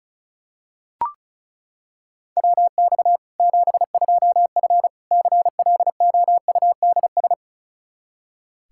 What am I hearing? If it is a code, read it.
Morse code, heard as a keyed tone, about 36 wpm, 700 Hz. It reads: WX 72F CLOUDS
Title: K1USN SST QSO Practice 35wpm